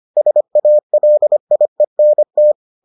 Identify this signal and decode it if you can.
Morse code, heard as a keyed tone, about 25 wpm, 600 Hz. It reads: SALIENT